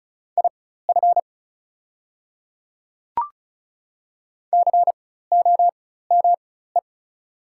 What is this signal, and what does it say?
Morse code, heard as a keyed tone, about 35 wpm, spaced as for 16 wpm, 700 Hz. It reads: IF COME
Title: 4x Character Spacing - Top 100 Words - 35wpm